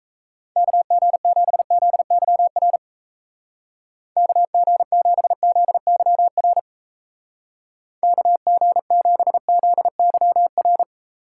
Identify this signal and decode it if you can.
Morse code, heard as a keyed tone, about 36 wpm, 700 Hz. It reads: KG7ZYR KG7ZYR KG7ZYR